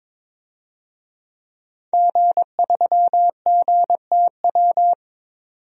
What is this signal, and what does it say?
Morse code, heard as a keyed tone, about 22 wpm, 700 Hz. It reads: G3GTW
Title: DXCC-100 Call Signs - 5 Characters - 22wpm